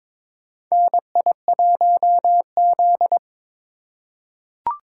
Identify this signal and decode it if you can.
Morse code, heard as a keyed tone, about 22 wpm, 700 Hz. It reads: NI1Z
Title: USA Call Signs - 4 Characters - 2x Character Spacing - 22wpm